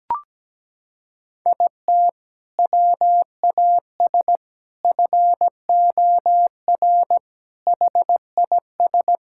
Morse code, heard as a keyed tone, about 17 wpm, 700 Hz. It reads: IT WAS FOR HIS
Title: Sentences from Top 1000 Words - Review Part 6 - 17wpm